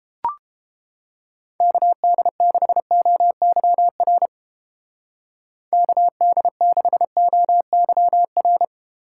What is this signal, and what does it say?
Morse code, heard as a keyed tone, about 30 wpm, 700 Hz. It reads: KD6OYR KD6OYR